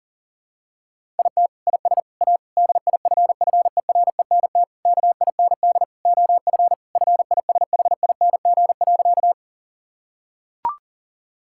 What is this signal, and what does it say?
Morse code, heard as a keyed tone, about 40 wpm, 700 Hz. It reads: IT IS A DIFFERENT KIND OF FISHING.
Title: Sentences from Top 1000 Words - Review Part 5 - 40wpm